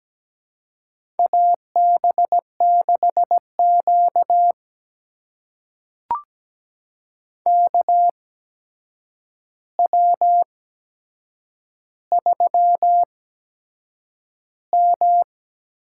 Morse code, heard as a keyed tone, about 17 wpm, spaced as for 7 wpm, 700 Hz. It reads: AB6Q KW3M
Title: USA Call Signs - 4 Characters - 8x Character Spacing - 17wpm